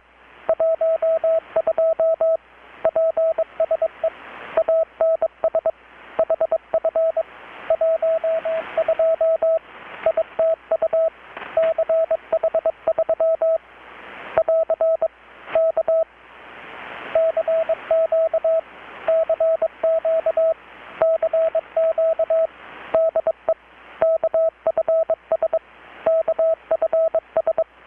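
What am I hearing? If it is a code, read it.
Morse code, heard as a keyed tone, about 21 wpm, 650 Hz, with radio noise. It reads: 12 PSE ANS HF 12 ITU CH3 <AR> K CQ CQ CQ DE KFS KFS
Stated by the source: Morse from KPH's Night of Nights XX maritime radiotelegraphy event, received in Chiapas, Mexico about 3,800 km away on a Kenwood R-600 with a grounded 10 m long-wire antenna.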